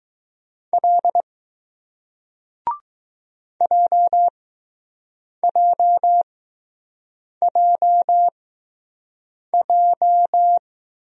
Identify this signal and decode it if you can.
Morse code, heard as a keyed tone, about 21 wpm, spaced as for 6 wpm, 700 Hz. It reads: L JJJJ